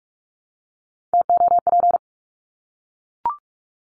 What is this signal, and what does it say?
Morse code, heard as a keyed tone, about 44 wpm, 700 Hz. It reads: TOP
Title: Top 500 Words Repeated 45wpm